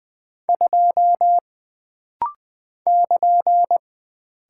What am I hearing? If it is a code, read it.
Morse code, heard as a keyed tone, about 20 wpm, 700 Hz. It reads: 2 <KN>
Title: Single Character (Letter:Number:Punctuation:Prosign) - Rapid-Fire - 20wpm